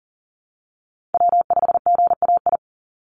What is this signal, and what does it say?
Morse code, heard as a keyed tone, about 40 wpm, 700 Hz. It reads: W5GAI
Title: Call Signs Repeated 2x - 40wpm